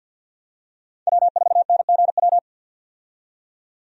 Morse code, heard as a keyed tone, about 45 wpm, 700 Hz. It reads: W4NGW
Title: USA Call Signs - 5 Characters - 50wpm